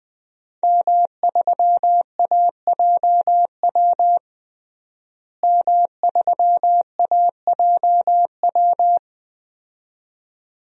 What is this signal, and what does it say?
Morse code, heard as a keyed tone, about 20 wpm, 700 Hz. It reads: M3AJW M3AJW
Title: DXCC-100 Call Signs - 5 Characters - Repeated 2x - 20wpm